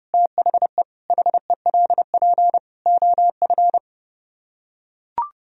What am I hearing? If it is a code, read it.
Morse code, heard as a keyed tone, about 30 wpm, 700 Hz. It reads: THE HELP OF